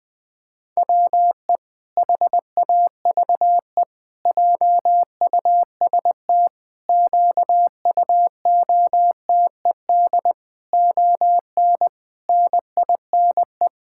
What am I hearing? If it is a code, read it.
Morse code, heard as a keyed tone, about 20 wpm, 700 Hz. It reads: WE HAVE JUST QUOTED ON NINE
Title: Pangram Challenge v2 - 20wpm